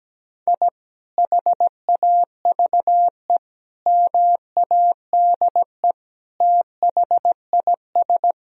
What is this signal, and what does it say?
Morse code, heard as a keyed tone, about 17 wpm, 700 Hz. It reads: I HAVE MADE THIS
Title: Sentences from Top 1000 Words - Review Part 2 - 17wpm